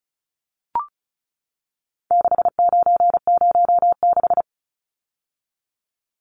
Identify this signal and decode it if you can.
Morse code, heard as a keyed tone, about 34 wpm, 700 Hz. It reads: 6906
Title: Numbers - 4-Digit 35wpm